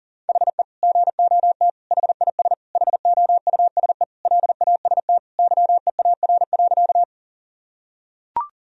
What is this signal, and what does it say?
Morse code, heard as a keyed tone, about 40 wpm, 700 Hz. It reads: HE GOT HIS HOUSE LAST YEAR.
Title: Sentences from Top 1000 Words - Review Part 1 - 40wpm